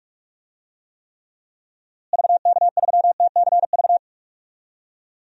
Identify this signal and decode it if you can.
Morse code, heard as a keyed tone, about 42 wpm, 700 Hz. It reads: VK3TCV
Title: DXCC-100 Call Signs - 6 Characters - Repeated 2x - 45wpm